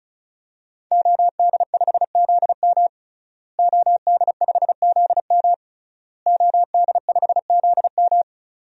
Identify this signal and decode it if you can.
Morse code, heard as a keyed tone, about 34 wpm, 700 Hz. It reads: OD5ZM OD5ZM OD5ZM
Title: DXCC-100 Call Signs - 5 Characters - Repeated 3x - 35wpm